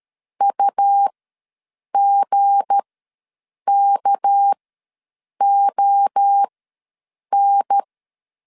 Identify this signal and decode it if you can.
Morse code, heard as a keyed tone, about 13 wpm, spaced as for 7 wpm, 800 Hz. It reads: UGKON
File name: LEZIONI / 7 GF9Z / 7 G ws 12wpm 3140